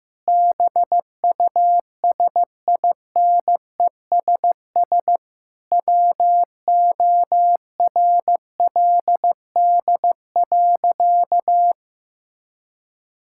Morse code, heard as a keyed tone, about 15 wpm, 700 Hz. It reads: BUSINESS WORLD.
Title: Sentences from Top 1000 Words - Review Part 6 - 15wpm